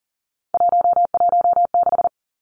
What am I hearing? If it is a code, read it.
Morse code, heard as a keyed tone, about 41 wpm, 700 Hz. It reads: 116
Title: Three Challenging Letter-Number Combinations 40wpm